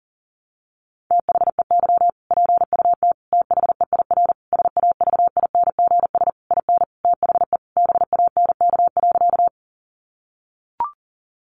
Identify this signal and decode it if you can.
Morse code, heard as a keyed tone, about 40 wpm, 700 Hz. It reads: THEY PUT THEIR SAVINGS IN THE BANK.
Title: Sentences from Top 1200 Words 40wpm